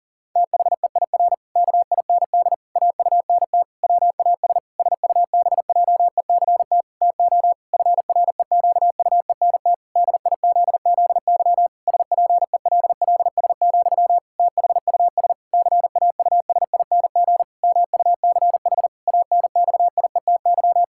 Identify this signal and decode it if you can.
Morse code, heard as a keyed tone, about 40 wpm, 700 Hz. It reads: THEIR KIND AUNT WAS SUBJECT TO FREQUENT DIZZY SPELLS, THUS CAUSING MUCH ANXIETY
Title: Pangram Challenge v2 - 40wpm